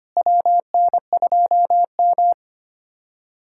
Morse code, heard as a keyed tone, about 25 wpm, 700 Hz. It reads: WN2M
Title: USA Callsign DE Callsign - 4 Characters - 25wpm